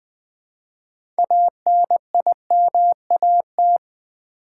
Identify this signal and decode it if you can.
Morse code, heard as a keyed tone, about 20 wpm, 700 Hz. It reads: ANIMAT
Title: Build A Word - 3 Letters At A Time - 9-Letter Words - 20wpm